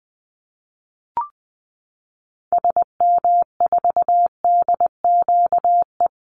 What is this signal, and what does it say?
Morse code, heard as a keyed tone, about 20 wpm, 700 Hz. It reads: SM4DQE
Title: Call Signs Repeated 3x - 20wpm